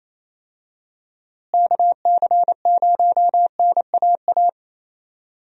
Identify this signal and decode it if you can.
Morse code, heard as a keyed tone, about 28 wpm, 700 Hz. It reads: KC0NAA
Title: USA Call Signs - 6 Characters - Repeated 3x - 28wpm